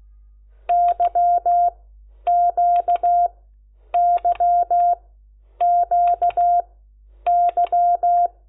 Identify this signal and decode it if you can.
Morse code, heard as a keyed tone, about 16 wpm, spaced as for 9 wpm, 700 Hz. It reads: YQYQY
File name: Lecciones / CLASE Nº10